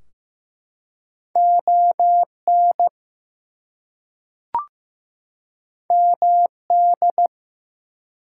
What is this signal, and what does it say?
Morse code, heard as a keyed tone, about 15 wpm, 700 Hz. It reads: ON MD
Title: US States and Canadian Provinces 15wpm